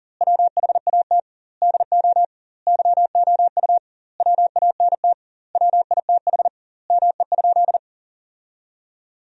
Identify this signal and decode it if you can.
Morse code, heard as a keyed tone, about 40 wpm, 700 Hz. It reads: WHAT DO YOU WANT WITH ME?
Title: 2x Word-Spacing - Sentences from Top 200 Words - 40wpm